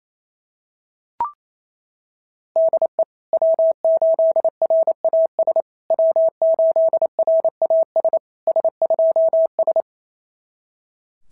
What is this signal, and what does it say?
Morse code, heard as a keyed tone, about 28 wpm, 690 Hz. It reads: DE W8RAS W8RAS S2S
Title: SOTA QSO Practice 28wpm